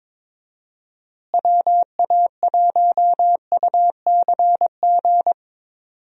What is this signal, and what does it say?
Morse code, heard as a keyed tone, about 22 wpm, 700 Hz. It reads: WA1UCG
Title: USA Call Signs - 6 Characters - Repeated 2x - 22wpm